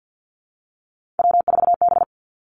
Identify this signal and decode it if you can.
Morse code, heard as a keyed tone, about 50 wpm, 700 Hz. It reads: W4B
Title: Three Challenging Letter-Number Combinations 50wpm